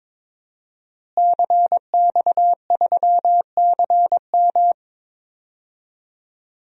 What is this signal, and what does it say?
Morse code, heard as a keyed tone, about 22 wpm, 700 Hz. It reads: CX3CM
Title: DXCC-100 Call Signs - 5 Characters - Repeated 3x - 22wpm